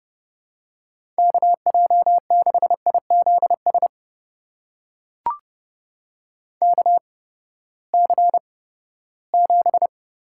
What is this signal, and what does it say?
Morse code, heard as a keyed tone, about 30 wpm, spaced as for 17 wpm, 700 Hz. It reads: KJ6IZS KC7
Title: USA Call Signs - 8x Character Spacing - 30wpm